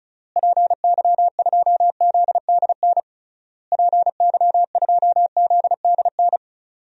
Morse code, heard as a keyed tone, about 36 wpm, 700 Hz. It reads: PY2ZDN PY2ZDN
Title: DXCC-100 Call Signs - 6 Characters - Repeated 3x - 35wpm